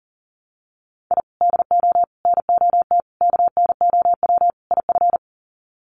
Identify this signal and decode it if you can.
Morse code, heard as a keyed tone, about 37 wpm, 700 Hz. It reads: I DO NOT KNOW IF
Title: Sets of 5 Words 40wpm